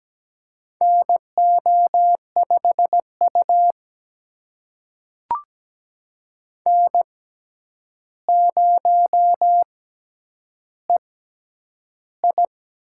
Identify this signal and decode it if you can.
Morse code, heard as a keyed tone, about 17 wpm, spaced as for 9 wpm, 700 Hz. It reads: NO5U N0EI
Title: USA Call Signs - 4 Characters - 6x Character Spacing - 17wpm